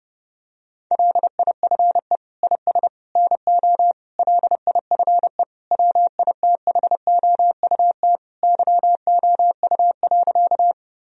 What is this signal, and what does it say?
Morse code, heard as a keyed tone, about 30 wpm, 700 Hz. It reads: LIFE IS NO LIFE WITHOUT YOU.
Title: Sentences from Top 1000 Words - Review Part 3 - 30wpm